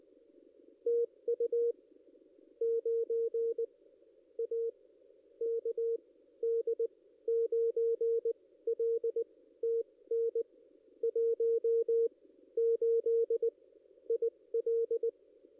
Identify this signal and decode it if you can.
Morse code, heard as a keyed tone, about 20 wpm, spaced as for 14 wpm, 440 Hz. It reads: TU 9 A KD9LTN 18 IL
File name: contests / ss-cw-k5tr-op-n5rz -1400